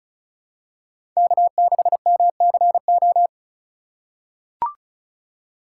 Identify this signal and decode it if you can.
Morse code, heard as a keyed tone, about 36 wpm, 700 Hz. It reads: K6MCO
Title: USA Call Signs - 5 Characters - 2x Character Spacing - 35wpm